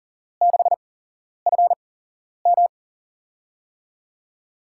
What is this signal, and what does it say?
Morse code, heard as a keyed tone, about 40 wpm, spaced as for 11 wpm, 700 Hz. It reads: 6FM
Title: USA Call Signs - 8x Character Spacing - 40wpm